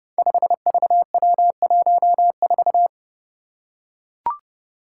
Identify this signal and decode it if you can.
Morse code, heard as a keyed tone, about 32 wpm, 700 Hz. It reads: 5VW14
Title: Five Challenging Letter-Number Combinations 30wpm